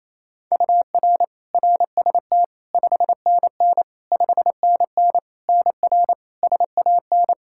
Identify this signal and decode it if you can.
Morse code, heard as a keyed tone, about 28 wpm, 700 Hz. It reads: UR RST 5NN 5NN NR SAN
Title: Practice QSO - Exchange 1 - Beginning - 28wpm